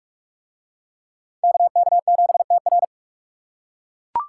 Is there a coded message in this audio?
Yes: Morse code from 1.4 to 2.8 s, heard as a keyed tone, about 40 wpm, 700 Hz. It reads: KK7TR